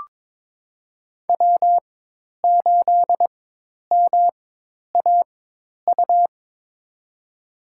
Morse code, heard as a keyed tone, about 21 wpm, spaced as for 10 wpm, 700 Hz. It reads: W8MAU
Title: USA Call Signs - 5 Characters - 4x Character Spacing - 22wpm